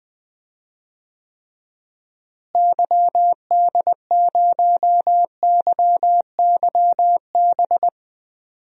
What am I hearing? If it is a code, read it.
Morse code, heard as a keyed tone, about 20 wpm, 700 Hz. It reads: YD0YYB
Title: DXCC-100 Call Signs - 6 Characters - Repeated 3x - 20wpm